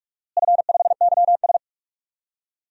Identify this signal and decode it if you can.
Morse code, heard as a keyed tone, about 46 wpm, 700 Hz. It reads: F5YS
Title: DXCC-100 Call Signs - 4 Characters - Repeated 2x - 45wpm